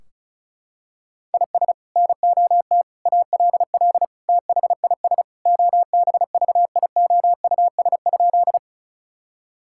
Morse code, heard as a keyed tone, about 36 wpm, 700 Hz. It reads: IS NOT ALL THIS OBVIOUS?
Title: Sentences from Top 1600 Words 35wpm